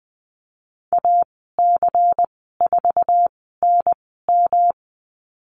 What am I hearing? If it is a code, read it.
Morse code, heard as a keyed tone, about 20 wpm, spaced as for 15 wpm, 700 Hz. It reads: AC4NM
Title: 173 Lesson 28g - Number 9 - Call Signs 20wpm